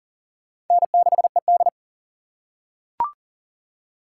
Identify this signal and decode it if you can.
Morse code, heard as a keyed tone, about 40 wpm, 700 Hz. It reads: N6ED
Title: USA Call Signs - 4 Characters - 2x Character Spacing - 40wpm